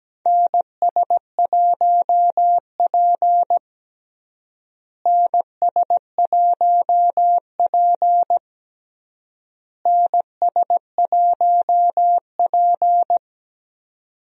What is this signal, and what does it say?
Morse code, heard as a keyed tone, about 17 wpm, 700 Hz. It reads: NS1P NS1P NS1P
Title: USA Call Signs - 4 Characters - Repeated 3x - 17wpm